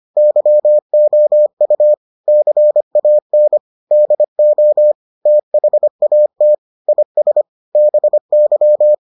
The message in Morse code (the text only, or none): YOU CAN DO THAT IS BY